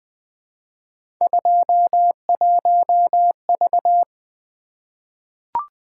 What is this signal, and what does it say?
Morse code, heard as a keyed tone, about 20 wpm, 700 Hz. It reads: 21V
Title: Three Letter-Number Combinations - 20wpm